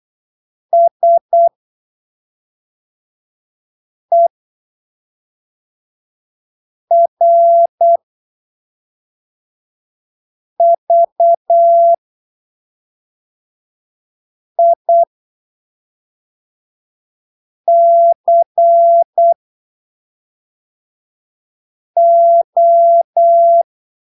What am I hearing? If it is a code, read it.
Morse code, heard as a keyed tone, about 8 wpm, spaced as for 3 wpm, 680 Hz. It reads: SERVICO
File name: arquivos / texto / texto 07 08ppm